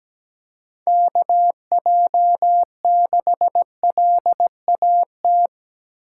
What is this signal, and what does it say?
Morse code, heard as a keyed tone, about 20 wpm, 700 Hz. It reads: KJ6LAT
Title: Speed Racing - USA 2x3 Call Signs 17wpm